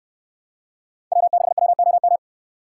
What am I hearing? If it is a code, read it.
Morse code, heard as a keyed tone, about 69 wpm, 700 Hz. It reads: W6WOM